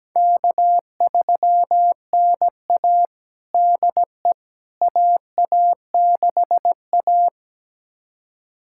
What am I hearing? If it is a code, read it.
Morse code, heard as a keyed tone, about 17 wpm, 700 Hz. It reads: K3NA DE AA6A
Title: USA Callsign DE Callsign - 4 Characters - 17wpm